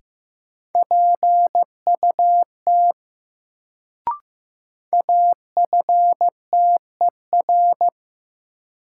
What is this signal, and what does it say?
Morse code, heard as a keyed tone, about 15 wpm, 700 Hz. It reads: PUT AFTER
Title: Sentences from Top 200 Words - New Word Drill - 15wpm